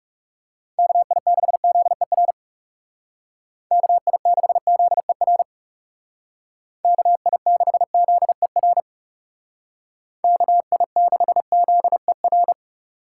Morse code, heard as a keyed tone, about 39 wpm, 700 Hz. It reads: KI6ZER KI6ZER KI6ZER KI6ZER